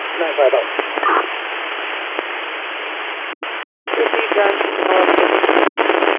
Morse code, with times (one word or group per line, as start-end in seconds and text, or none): none